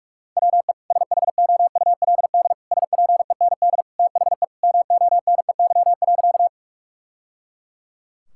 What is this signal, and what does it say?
Morse code, heard as a keyed tone, about 46 wpm, 700 Hz. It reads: WE SHOULD SPEND THE MONEY.
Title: Sentences from Top 1000 Words - Review Part 5 - 45wpm